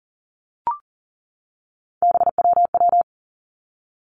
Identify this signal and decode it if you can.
Morse code, heard as a keyed tone, about 40 wpm, 700 Hz. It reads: BWW
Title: Three Challenging Letter-Number Combinations 40wpm